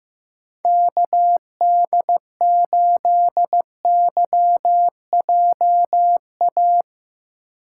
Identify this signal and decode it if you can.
Morse code, heard as a keyed tone, about 15 wpm, 700 Hz. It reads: KD8YJA
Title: USA Call Signs - 2x Character Spacing - 15wpm